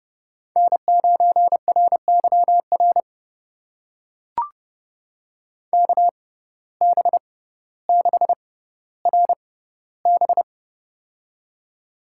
Morse code, heard as a keyed tone, about 30 wpm, spaced as for 15 wpm, 700 Hz. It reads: N9RYR KB6RB
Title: USA Call Signs - 6x Character Spacing - 30wpm